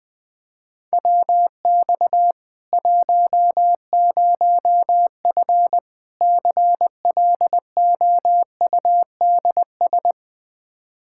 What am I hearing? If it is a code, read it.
Morse code, heard as a keyed tone, about 20 wpm, 700 Hz. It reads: WX 10F CLOUDS
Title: Practice QSO - Exchange 2 - Info - 20wpm